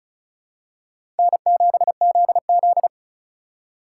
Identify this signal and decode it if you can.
Morse code, heard as a keyed tone, about 36 wpm, 700 Hz. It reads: N7ZZ
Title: USA Call Signs - 3x Character Spacing - 35wpm